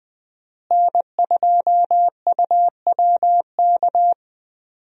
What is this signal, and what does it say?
Morse code, heard as a keyed tone, about 20 wpm, 700 Hz. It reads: N2UWK
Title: USA Call Signs - 5 Characters - Repeated 3x - 20wpm